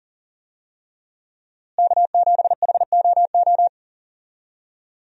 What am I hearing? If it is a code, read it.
Morse code, heard as a keyed tone, about 40 wpm, 700 Hz. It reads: K7HOO
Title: USA Call Signs - 5 Characters - Repeated 3x - 40wpm